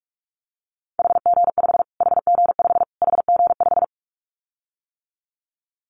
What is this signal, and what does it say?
Morse code, heard as a keyed tone, about 44 wpm, 700 Hz. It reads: HG5 HG5 HG5
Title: Callsign Prefixes for Top 25 DXCC - Repeated 3x - 45wpm